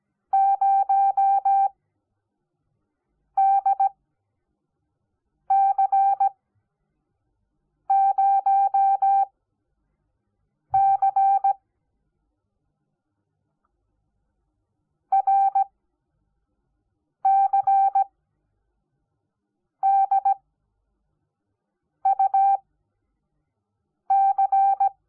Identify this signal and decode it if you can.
Morse code, heard as a keyed tone, about 17 wpm, spaced as for 5 wpm, 780 Hz. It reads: 0DC0C RCDUC
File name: Onlinelehrgang / morsen / DJ4UF Morsen05